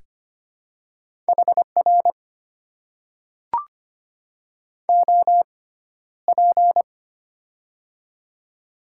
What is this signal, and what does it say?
Morse code, heard as a keyed tone, about 25 wpm, spaced as for 13 wpm, 700 Hz. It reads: HR OP
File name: Top 90 QSO Elements - 6x Character Spacing - 25wpm